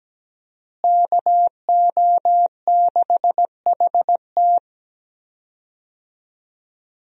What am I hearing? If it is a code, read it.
Morse code, heard as a keyed tone, about 17 wpm, 700 Hz. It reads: KO6HT
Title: USA Call Signs - 5 Characters - Repeated 2x - 17wpm